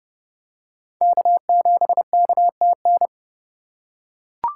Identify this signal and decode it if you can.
Morse code, heard as a keyed tone, about 26 wpm, 700 Hz. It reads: K7KTN